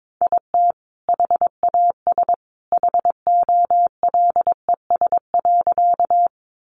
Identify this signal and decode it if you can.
Morse code, heard as a keyed tone, about 22 wpm, 700 Hz. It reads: IT HAS HOLES.
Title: Additional Easy Phrases 22wpm